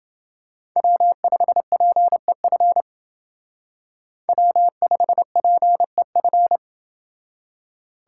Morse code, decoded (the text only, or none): W5PEF W5PEF